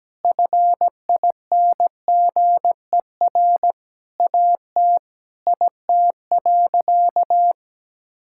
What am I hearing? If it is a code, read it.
Morse code, heard as a keyed tone, about 17 wpm, 700 Hz. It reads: FINGER AT IT.
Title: Sentences from Top 1600 Words 17wpm